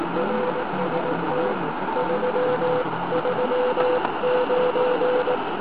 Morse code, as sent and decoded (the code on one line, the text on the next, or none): .-. ... - / ...-- ...-- ----.
RST 339